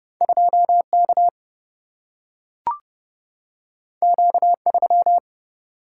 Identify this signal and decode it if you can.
Morse code, heard as a keyed tone, about 30 wpm, 700 Hz. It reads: 2 K Q 3
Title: Single-Letter Single-Number 30wpm